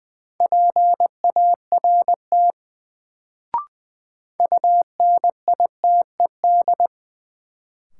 Morse code, heard as a keyed tone, about 20 wpm, 700 Hz. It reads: PART UNITED
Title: Sets of 2 Words - New Word Drill - 20wpm